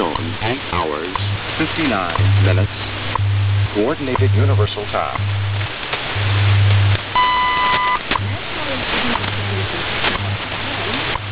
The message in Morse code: none